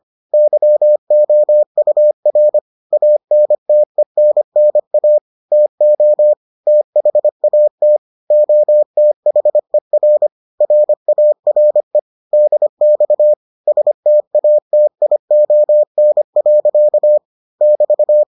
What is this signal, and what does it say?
Morse code, heard as a keyed tone, about 25 wpm, 600 Hz. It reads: YOUR ANTENNA TO THAT OTHER RARE DX STATION. <BT>